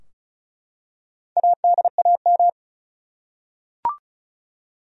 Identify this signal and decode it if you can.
Morse code, heard as a keyed tone, about 36 wpm, 700 Hz. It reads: ADAM
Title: USA Common Names - Max 4 Letters - 35wpm